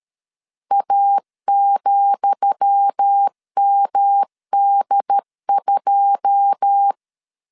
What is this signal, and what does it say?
Morse code, heard as a keyed tone, about 13 wpm, 800 Hz. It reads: A,MD2
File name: LEZIONI / 10 SIGNS / 10D COMA 12wpm 2544